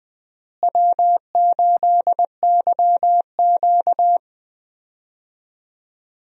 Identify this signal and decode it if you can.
Morse code, heard as a keyed tone, about 20 wpm, 700 Hz. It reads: W8YQ
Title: USA Call Signs - 4 Characters - Repeated 3x - 20wpm